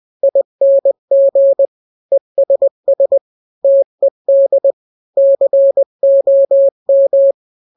Morse code, heard as a keyed tone, about 20 wpm, 550 Hz. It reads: ING ESS TED COM
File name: n-grams / 3-grams 20 18 WPM